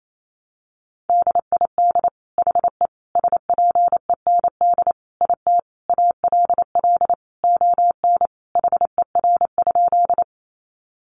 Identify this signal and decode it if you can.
Morse code, heard as a keyed tone, about 28 wpm, 700 Hz. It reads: DID HE SPEND IT ALL ON HER?
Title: Sentences from Top 800 Words 28wpm